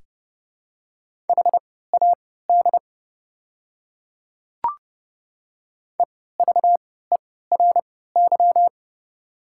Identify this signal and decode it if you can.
Morse code, heard as a keyed tone, about 28 wpm, spaced as for 17 wpm, 690 Hz. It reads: HAD EVERY
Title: 3x Character Spacing - Top 200 Words - 30wpm